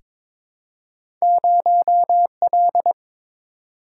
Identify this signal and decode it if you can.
Morse code, heard as a keyed tone, about 23 wpm, 700 Hz. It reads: 0L
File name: practice / 22wpm / Two Letter-Number Combinations - 22wpm